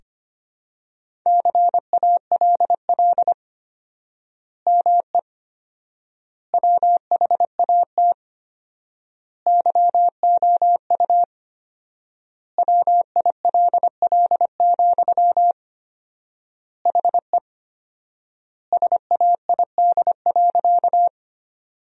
Morse code, heard as a keyed tone, about 25 wpm, 700 Hz. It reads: CALL ME WHAT YOU WILL, HE SAID.
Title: 4x Word-Spacing - Sentences from Top 200 Words - 25wpm